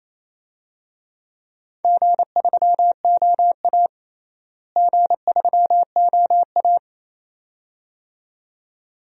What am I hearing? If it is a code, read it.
Morse code, heard as a keyed tone, about 28 wpm, 700 Hz. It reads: G3OA G3OA
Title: DXCC-100 Call Signs - 4 Characters - Repeated 2x - 28wpm